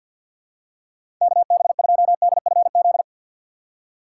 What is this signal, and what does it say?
Morse code, heard as a keyed tone, about 48 wpm, 700 Hz. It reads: KB2DFZ
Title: USA Call Signs - 2x Character Spacing - 50wpm